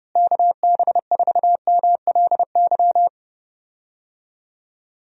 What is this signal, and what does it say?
Morse code, heard as a keyed tone, about 32 wpm, 700 Hz. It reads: KB4MLY
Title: USA Call Signs - 6 Characters - Repeated 2x - 30wpm